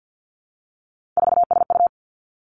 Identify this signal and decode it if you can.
Morse code, heard as a keyed tone, about 50 wpm, 700 Hz. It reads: 4SU
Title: Three Challenging Letter-Number Combinations 50wpm